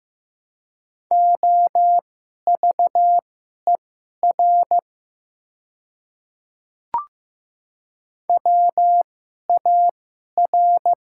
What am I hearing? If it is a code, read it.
Morse code, heard as a keyed tone, about 15 wpm, spaced as for 11 wpm, 700 Hz. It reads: OVER WAR
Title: 2x Character Spacing - Top 500 Words - 15wpm